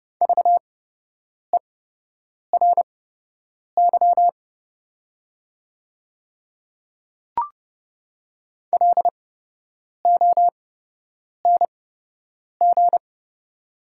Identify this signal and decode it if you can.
Morse code, heard as a keyed tone, about 31 wpm, spaced as for 8 wpm, 700 Hz. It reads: VERY LONG
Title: 8x Character Spacing - Top 300 Words - 30wpm